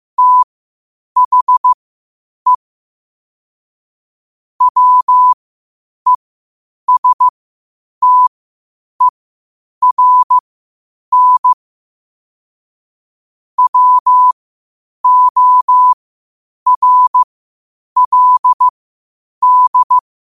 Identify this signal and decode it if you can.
Morse code, heard as a keyed tone, about 15 wpm, spaced as for 8 wpm, 1000 Hz. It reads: THE WESTERN WORLD